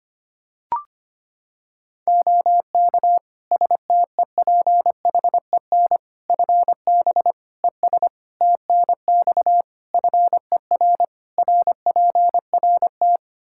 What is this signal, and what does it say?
Morse code, heard as a keyed tone, about 25 wpm, 700 Hz. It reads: OK STEPHEN FB ES TNX FER RPRT
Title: Practice QSO - Exchange 2 - Info - 25wpm